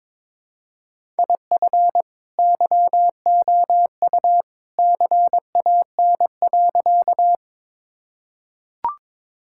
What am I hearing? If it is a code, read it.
Morse code, heard as a keyed tone, about 22 wpm, 700 Hz. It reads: IF YOU CAN.
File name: practice / 22wpm / Sentences with 2-Word Phrases - 22wpm